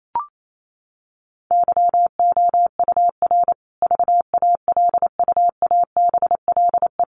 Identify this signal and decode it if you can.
Morse code, heard as a keyed tone, about 28 wpm, 700 Hz. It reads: YOUR VALUABLE
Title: Sentences from Top 1300 Words 28wpm